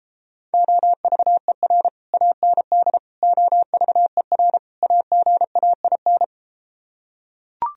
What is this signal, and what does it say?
Morse code, heard as a keyed tone, about 33 wpm, 700 Hz. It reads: OVER AND OVER AGAIN